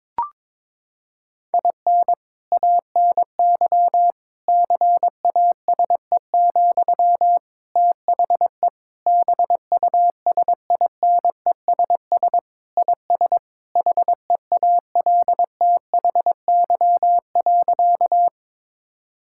Morse code, heard as a keyed tone, about 22 wpm, 700 Hz. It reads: IN ANY CASE, THE BUSINESS IS HEALTHY.
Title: Sentences from Top 1000 Words - Review Part 5 - 22wpm